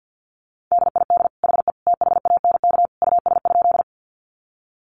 Morse code, heard as a keyed tone, about 50 wpm, 700 Hz. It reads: DID HE THANK US?
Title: Sentences from Top 600 Words 50wpm